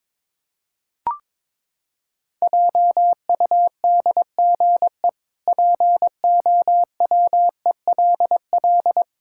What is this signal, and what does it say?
Morse code, heard as a keyed tone, about 21 wpm, 700 Hz. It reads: JUDGE POWELL
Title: Pangram Challenge v2 - 22wpm